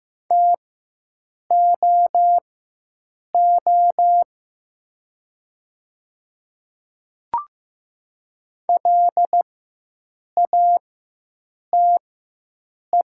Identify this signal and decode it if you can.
Morse code, heard as a keyed tone, about 15 wpm, spaced as for 7 wpm, 700 Hz. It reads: TOO LATE
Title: 4x Character Spacing - Top 300 Words - 15wpm